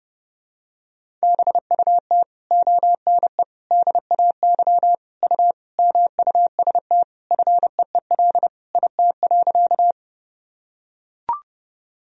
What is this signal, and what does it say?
Morse code, heard as a keyed tone, about 30 wpm, 700 Hz. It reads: BUT ONE DAY U MUST FEEL IT.
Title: Sentences from Top 400 Words and Abbreviations 30wpm